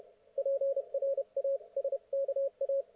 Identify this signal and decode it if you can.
Morse code, heard as a keyed tone, about 31 wpm, spaced as for 27 wpm, 530 Hz. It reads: PRASKA
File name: Archiv / CUC107 / ok1fhi cul107